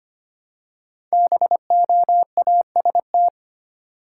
Morse code, heard as a keyed tone, about 25 wpm, 700 Hz. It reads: BOAST
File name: 5 Letter Words - First 3 Letters Given - 25wpm